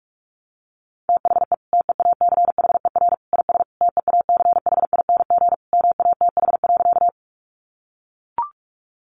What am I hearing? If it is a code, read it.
Morse code, heard as a keyed tone, about 44 wpm, 700 Hz. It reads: THE TEACHER IS TEACHING MATH.
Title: Easy Phrases 45wpm